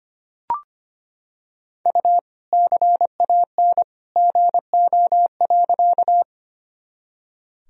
Morse code, heard as a keyed tone, about 25 wpm, 700 Hz. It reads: U CAN GO.
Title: Sentences from Top 100 Words and Abbreviations 25wpm